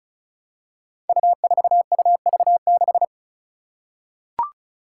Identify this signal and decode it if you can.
Morse code, heard as a keyed tone, about 35 wpm, 700 Hz. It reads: U4UV6
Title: Five Challenging Letter-Number Combinations 35wpm